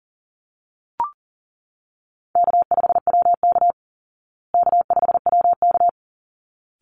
Morse code, heard as a keyed tone, about 40 wpm, 700 Hz. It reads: K5WK K5WK
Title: Easy Call Signs Repeated 2x - 40wpm